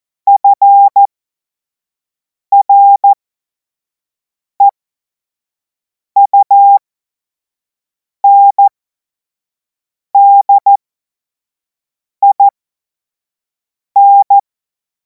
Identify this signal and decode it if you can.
Morse code, heard as a keyed tone, about 14 wpm, spaced as for 5 wpm, 800 Hz. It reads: FREUNDIN
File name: Morsekurs / Witze / Witz 17. März 2026 25 F70